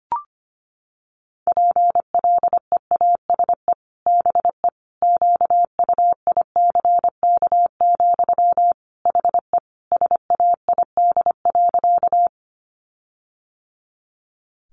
Morse code, heard as a keyed tone, about 25 wpm, 700 Hz. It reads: PLEASE BE QUICK, HE SAID.
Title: Sentences from Top 1500 Words 25wpm